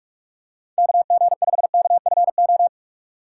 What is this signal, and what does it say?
Morse code, heard as a keyed tone, about 46 wpm, 700 Hz. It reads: KG5KFO
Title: USA Call Signs - 3x Character Spacing - 45wpm